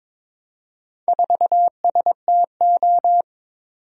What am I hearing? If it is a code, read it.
Morse code, heard as a keyed tone, about 22 wpm, 700 Hz. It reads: 4STO
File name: practice / 22wpm / Four Letter-Number Combinations - 22wpm